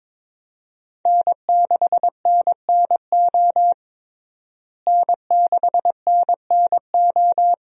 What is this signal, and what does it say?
Morse code, heard as a keyed tone, about 22 wpm, 700 Hz. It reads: N6NNO N6NNO
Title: USA Call Signs - 5 Characters - Repeated 3x - 22wpm